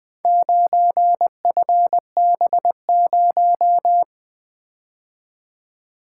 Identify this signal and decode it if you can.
Morse code, heard as a keyed tone, about 23 wpm, 700 Hz. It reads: 9FB0
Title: Speed-Racing - Four Letter-Number Combinations - 20wpm